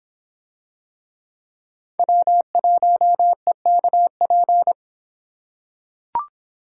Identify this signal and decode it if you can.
Morse code, heard as a keyed tone, about 26 wpm, 700 Hz. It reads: W1EKP